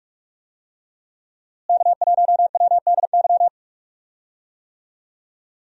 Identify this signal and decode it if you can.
Morse code, heard as a keyed tone, about 44 wpm, 700 Hz. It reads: K1WDY
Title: USA Call Signs - 5 Characters - Repeated 2x - 45wpm